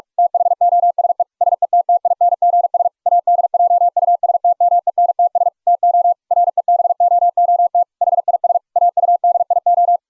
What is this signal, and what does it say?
Morse code, heard as a keyed tone, about 44 wpm, 690 Hz. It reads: THOSE SETTINGS ADJUSTMENTS TO REBOOT HIS AUDIO